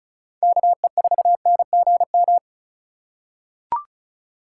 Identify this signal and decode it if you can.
Morse code, heard as a keyed tone, about 36 wpm, 700 Hz. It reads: KE4NGM
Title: USA Call Signs - 6 Characters - 2x Character Spacing - 35wpm